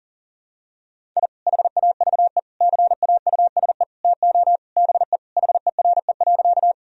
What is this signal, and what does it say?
Morse code, heard as a keyed tone, about 40 wpm, 700 Hz. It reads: I HAVE CAUSE TO BE HERE.
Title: Sentences from Top 1000 Words - Review Part 1 - 40wpm